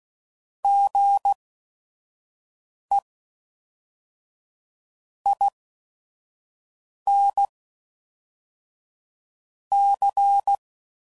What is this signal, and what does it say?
Morse code, heard as a keyed tone, about 16 wpm, 800 Hz. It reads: G E I N C